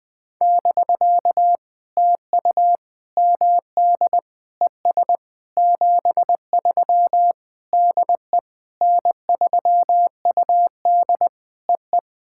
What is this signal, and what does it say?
Morse code, heard as a keyed tone, about 20 wpm, 700 Hz. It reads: <BK> TU MD ES 73 DE N3UD EE
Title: POTA QSO Practice - 20wpm